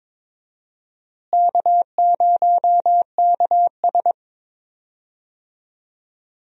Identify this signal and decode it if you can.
Morse code, heard as a keyed tone, about 22 wpm, 700 Hz. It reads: K0KS
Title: USA Call Signs - 4 Characters - Repeated 2x - 22wpm